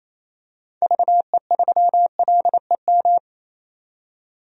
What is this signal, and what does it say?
Morse code, heard as a keyed tone, about 28 wpm, 700 Hz. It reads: VE3LEM
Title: DXCC-100 Call Signs - 6 Characters - 28wpm